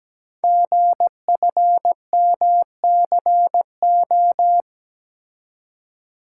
Morse code, heard as a keyed tone, about 17 wpm, 700 Hz. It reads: GFMCO
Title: Code Groups - 5 Letters - 17wpm